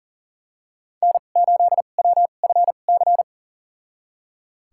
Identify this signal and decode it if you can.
Morse code, heard as a keyed tone, about 40 wpm, spaced as for 29 wpm, 700 Hz. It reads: N8WFC
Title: USA Call Signs - 5 Characters - 2x Character Spacing - 40wpm